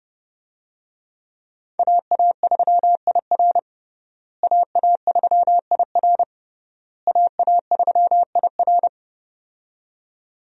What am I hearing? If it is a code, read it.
Morse code, heard as a keyed tone, about 30 wpm, 700 Hz. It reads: AA3IR AA3IR AA3IR
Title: USA Call Signs - 5 Characters - Repeated 3x - 30wpm